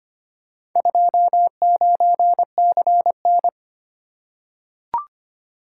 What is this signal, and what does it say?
Morse code, heard as a keyed tone, about 25 wpm, 700 Hz. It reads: 29CN
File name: practice / 25wpm / Four Letter-Number Combinations - 25wpm